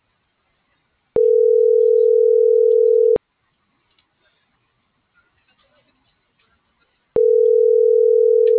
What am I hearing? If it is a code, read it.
Ringback tone (North America), 440 + 480 Hz.